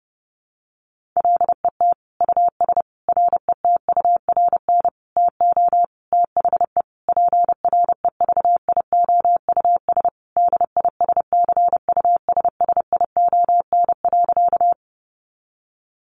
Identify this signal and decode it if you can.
Morse code, heard as a keyed tone, about 30 wpm, 700 Hz. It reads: LET US RETURN TO THE PREVIOUS DISCUSSION.
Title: Sentences from Top 1500 Words 30wpm